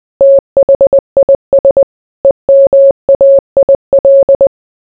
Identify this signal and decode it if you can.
Morse code, heard as a keyed tone, about 20 wpm, 550 Hz. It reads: THIS EMAIL